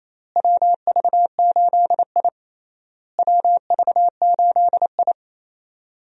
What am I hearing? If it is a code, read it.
Morse code, heard as a keyed tone, about 28 wpm, 700 Hz. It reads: WV8I WV8I
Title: USA Call Signs - 4 Characters - Repeated 3x - 28wpm